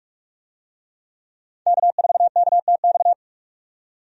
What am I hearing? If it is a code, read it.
Morse code, heard as a keyed tone, about 44 wpm, 700 Hz. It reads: K4KTX
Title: USA Call Signs - 5 Characters - 45wpm